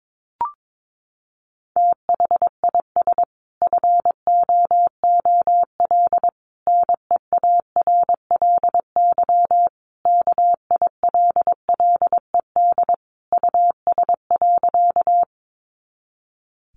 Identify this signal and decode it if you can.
Morse code, heard as a keyed tone, about 22 wpm, 700 Hz. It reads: THIS FOOL NEARLY KILLED US.
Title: A Sound Of Thunder 22wpm